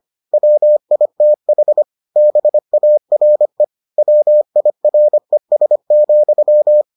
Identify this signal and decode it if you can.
Morse code, heard as a keyed tone, about 25 wpm, 600 Hz. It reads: WITH BARE WIRES,